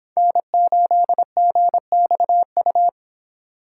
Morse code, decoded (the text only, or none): N8GXU